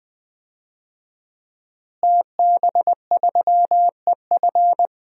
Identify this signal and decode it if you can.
Morse code, heard as a keyed tone, about 20 wpm, 700 Hz. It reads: TB3EF
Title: DXCC-100 Call Signs - 5 Characters - Repeated 3x - 20wpm